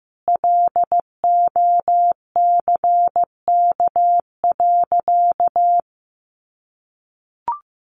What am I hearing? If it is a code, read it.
Morse code, heard as a keyed tone, about 15 wpm, 700 Hz. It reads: LOCK.
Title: Sentences from Top 1300 Words 15wpm